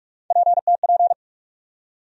Morse code, heard as a keyed tone, about 44 wpm, 700 Hz. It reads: PTP
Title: USA Call Sign Postfixes - 3 Letters 45wpm